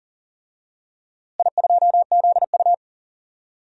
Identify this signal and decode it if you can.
Morse code, heard as a keyed tone, about 40 wpm, 700 Hz. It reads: I2ZU
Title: DXCC-100 Call Signs - 4 Characters - Repeated 3x - 40wpm